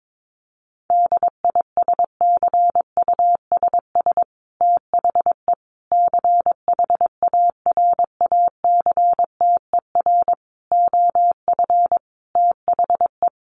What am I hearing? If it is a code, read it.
Morse code, heard as a keyed tone, about 22 wpm, 700 Hz. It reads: DISCUSS THE CHARACTER OF THE
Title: Sentences from Top 1500 Words 22wpm